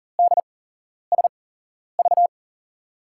Morse code, heard as a keyed tone, about 40 wpm, spaced as for 11 wpm, 700 Hz. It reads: DSV
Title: USA Call Signs - 5 Characters - 8x Character Spacing - 40wpm